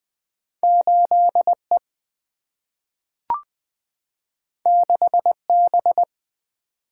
Morse code, heard as a keyed tone, about 20 wpm, 700 Hz. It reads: 8 E 6 B
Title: Single-Letter Single-Number 20wpm